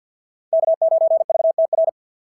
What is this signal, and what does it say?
Morse code, heard as a keyed tone, about 53 wpm, 690 Hz. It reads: K9VTR
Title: SOTA QSO Practice 50wpm